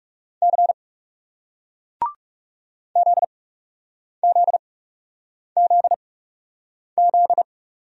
Morse code, heard as a keyed tone, about 39 wpm, spaced as for 8 wpm, 700 Hz. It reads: C ZZZZ